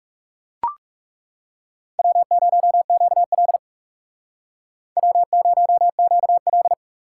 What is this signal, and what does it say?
Morse code, heard as a keyed tone, about 36 wpm, 700 Hz. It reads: W0QL W0QL
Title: Speed Racing - USA 1x2 Call Signs 30wpm